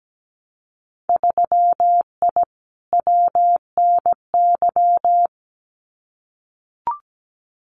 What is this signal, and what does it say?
Morse code, heard as a keyed tone, about 17 wpm, 700 Hz. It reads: 3I WNY
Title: Winter Field Day Exchanges 17wpm